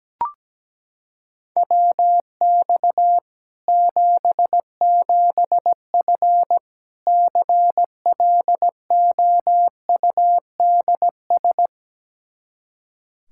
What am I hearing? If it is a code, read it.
Morse code, heard as a keyed tone, about 17 wpm, 700 Hz. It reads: WX 77F CLOUDS
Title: Practice QSO - Exchange 2 - Info - 17wpm